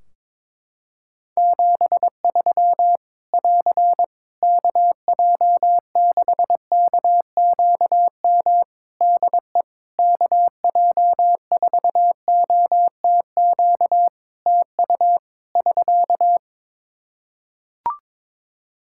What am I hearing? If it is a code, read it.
Morse code, heard as a keyed tone, about 22 wpm, 700 Hz. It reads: 73 <AR> KJ6KQM DE KJ4OTQ TU <SK>
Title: Practice QSO - Exchange 4 - Ending - 22wpm